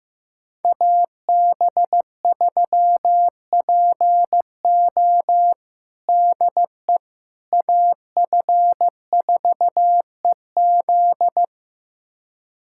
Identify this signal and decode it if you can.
Morse code, heard as a keyed tone, about 15 wpm, 700 Hz. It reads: AB3PO DE AF4EZ
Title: USA Callsign DE Callsign - 5 Characters - 15wpm